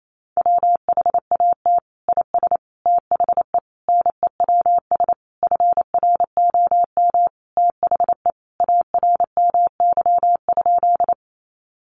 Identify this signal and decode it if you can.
Morse code, heard as a keyed tone, about 28 wpm, 700 Hz. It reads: WHAT IS THE NEWS FROM THE ARMY?
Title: Sentences from Top 800 Words 28wpm